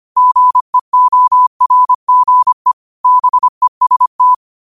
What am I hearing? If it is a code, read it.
Morse code, heard as a keyed tone, about 25 wpm, 1000 Hz. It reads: GEORGE BEST
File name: qotdfiles / 25wpm